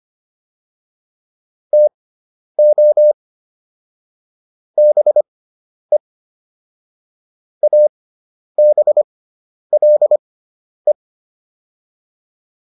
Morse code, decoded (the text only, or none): TO BE ABLE